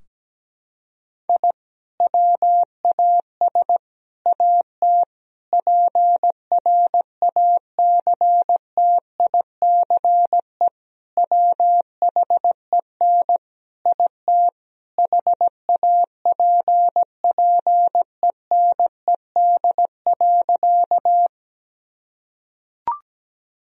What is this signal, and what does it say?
Morse code, heard as a keyed tone, about 17 wpm, 700 Hz. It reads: I WAS AT PRACTICE WHEN IT HAPPENED.
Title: Sentences from Top 1000 Words - Review Part 4 - 17wpm